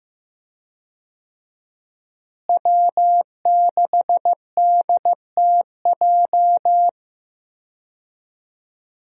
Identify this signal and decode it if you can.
Morse code, heard as a keyed tone, about 15 wpm, 700 Hz. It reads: W6DTJ
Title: USA Call Signs - 5 Characters - Repeated 2x - 15wpm